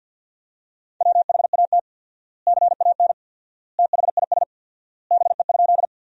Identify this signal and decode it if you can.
Morse code, heard as a keyed tone, about 52 wpm, 700 Hz. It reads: WHAT CAN THIS BE?
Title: 4x Word-Spacing - Sentences from Top 100 Words - 50wpm